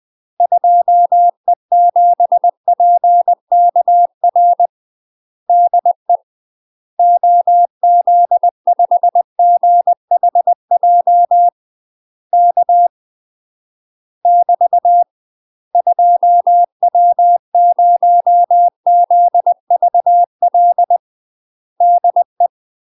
Morse code, heard as a keyed tone, about 20 wpm, 700 Hz. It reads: 2E7PKR DE OZ5GHJ K <BT> 2W0ZVL DE
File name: UK MASTER QSOs TRAINING / 20WPM / UK MASTER QSOs 0068